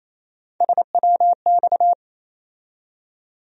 Morse code, heard as a keyed tone, about 28 wpm, 700 Hz. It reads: SWX
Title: USA Call Sign Postfixes - 3 Letters 28wpm